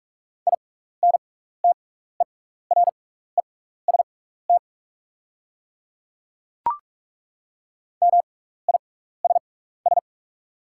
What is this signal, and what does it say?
Morse code, heard as a keyed tone, about 44 wpm, spaced as for 15 wpm, 700 Hz. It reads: INTEREST MISS
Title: 6x Character Spacing - Top 500 Words - 45wpm